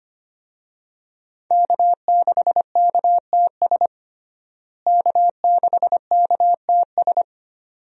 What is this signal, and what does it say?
Morse code, heard as a keyed tone, about 25 wpm, 700 Hz. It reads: K6KTS K6KTS
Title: USA Call Signs - 5 Characters - Repeated 2x - 25wpm